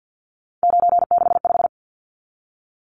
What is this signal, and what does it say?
Morse code, heard as a keyed tone, about 50 wpm, 700 Hz. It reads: 965
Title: Numbers - 3-Digit 50wpm